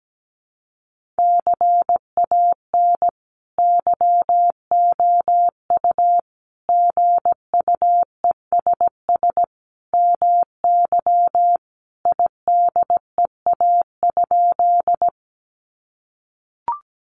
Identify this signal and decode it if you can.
Morse code, heard as a keyed tone, about 17 wpm, 700 Hz. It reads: CAN YOU GUESS MY IDEA?
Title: Sentences from Top 1500 Words 17wpm